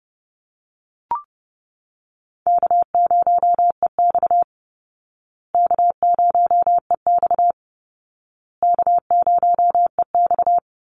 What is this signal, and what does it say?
Morse code, heard as a keyed tone, about 30 wpm, 700 Hz. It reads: K0EX K0EX K0EX
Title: Call Signs Repeated 2x - 30wpm